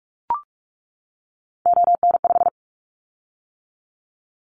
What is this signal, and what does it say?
Morse code, heard as a keyed tone, about 44 wpm, 700 Hz. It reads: ON5
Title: Callsign Prefixes for Top 100 DXCC - 45wpm